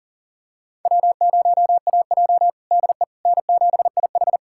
Morse code, heard as a keyed tone, about 40 wpm, 700 Hz. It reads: W0AJ DE N7IH
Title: USA Callsign DE Callsign - 4 Characters - 40wpm